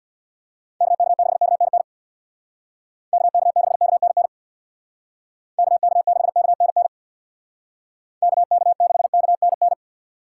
Morse code, heard as a keyed tone, about 65 wpm, 700 Hz. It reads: KK6KNN KK6KNN KK6KNN KK6KNN